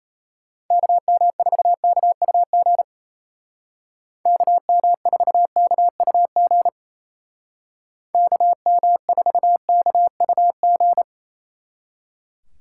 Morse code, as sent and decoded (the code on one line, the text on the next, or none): -.- -- ....- -.- ..- --. / -.- -- ....- -.- ..- --. / -.- -- ....- -.- ..- --.
KM4KUG KM4KUG KM4KUG